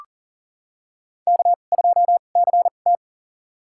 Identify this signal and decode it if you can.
Morse code, heard as a keyed tone, about 40 wpm, spaced as for 29 wpm, 700 Hz. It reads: K2CT
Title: USA Call Signs - 4 Characters - 2x Character Spacing - 40wpm